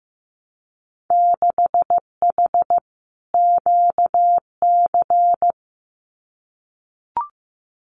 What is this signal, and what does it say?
Morse code, heard as a keyed tone, about 15 wpm, 700 Hz. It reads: 6H QC
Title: Winter Field Day Exchanges Encore 15wpm